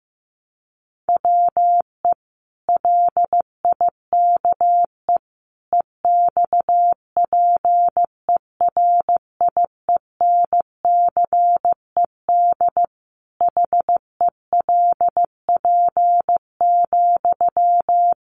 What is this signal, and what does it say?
Morse code, heard as a keyed tone, about 15 wpm, 700 Hz. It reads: WE LIKE EXPERIENCED HELP,
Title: Sentences from Top 1100 Words 15wpm